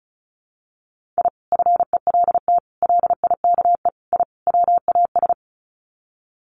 Morse code, heard as a keyed tone, about 34 wpm, 700 Hz. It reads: I FELT LIKE I WAS
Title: Sets of 5 Words Encore 35wpm